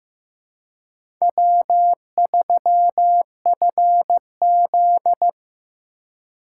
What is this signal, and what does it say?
Morse code, heard as a keyed tone, about 15 wpm, 700 Hz. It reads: W3FZ
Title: USA Call Signs - 4 Characters - 4x Character Spacing - 15wpm